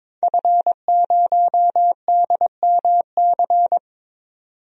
Morse code, heard as a keyed tone, about 22 wpm, 700 Hz. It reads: F0DMC
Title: DXCC-100 Call Signs - 5 Characters - 22wpm